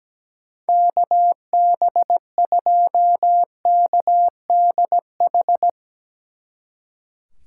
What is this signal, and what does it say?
Morse code, heard as a keyed tone, about 17 wpm, 700 Hz. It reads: KB2KDH
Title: Speed Racing - USA 2x3 Call Signs 17wpm